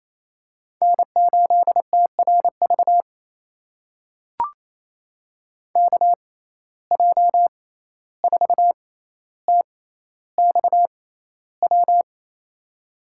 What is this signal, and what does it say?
Morse code, heard as a keyed tone, about 28 wpm, spaced as for 14 wpm, 700 Hz. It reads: N8TRV KJ4TXW
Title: USA Call Signs - 6x Character Spacing - 28wpm